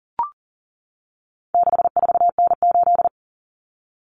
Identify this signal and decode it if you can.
Morse code, heard as a keyed tone, about 40 wpm, 700 Hz. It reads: 64N8
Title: Common Cut Numbers - 4-Digit 40wpm